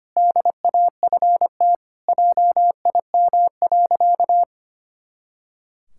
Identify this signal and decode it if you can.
Morse code, heard as a keyed tone, about 25 wpm, 700 Hz. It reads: DAFT JIM.
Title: Pangram Challenge v2 - 25wpm